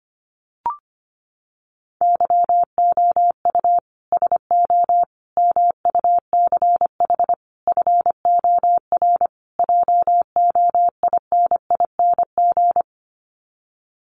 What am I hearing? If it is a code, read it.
Morse code, heard as a keyed tone, about 25 wpm, 700 Hz. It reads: YOU SO MUCH FOR JOINING
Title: Sets of 5 Words Encore2 25wpm